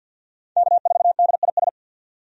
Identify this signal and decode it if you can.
Morse code, heard as a keyed tone, about 52 wpm, 700 Hz. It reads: K4DIS
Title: USA Call Signs - 5 Characters - 50wpm